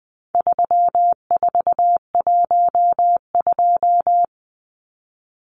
Morse code, heard as a keyed tone, about 20 wpm, 700 Hz. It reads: 3412
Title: Numbers - 4-Digit 20wpm